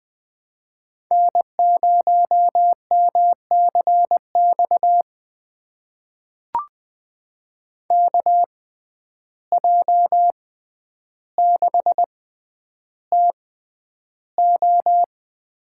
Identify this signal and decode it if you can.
Morse code, heard as a keyed tone, about 21 wpm, spaced as for 10 wpm, 700 Hz. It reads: N0MCX KJ6TO
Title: USA Call Signs - 5 Characters - 6x Character Spacing - 20wpm